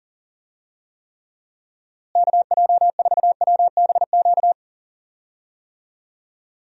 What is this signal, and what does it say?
Morse code, heard as a keyed tone, about 40 wpm, 700 Hz. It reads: KJ4WBQ
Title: USA Call Signs - 6 Characters - Repeated 2x - 40wpm